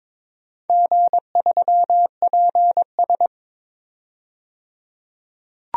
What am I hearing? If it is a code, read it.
Morse code, heard as a keyed tone, about 22 wpm, 700 Hz. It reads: G3PS
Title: DXCC-100 Call Signs - 4 Characters - 22wpm